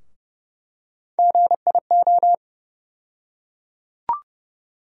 Morse code, heard as a keyed tone, about 30 wpm, 700 Hz. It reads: GIO
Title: USA Call Sign Postfixes - 3 Letters 30wpm